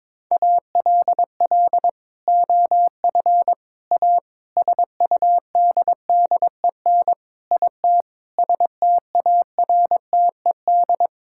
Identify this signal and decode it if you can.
Morse code, heard as a keyed tone, about 22 wpm, 700 Hz. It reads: ALL OF A SUDDEN IT STARTED
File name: practice / 22wpm / Sentences with 4-Word Phrases - 22wpm